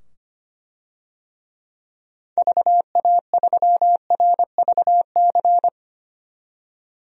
Morse code, heard as a keyed tone, about 25 wpm, 700 Hz. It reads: VA3RVC
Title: DXCC-100 Call Signs - 6 Characters - 25wpm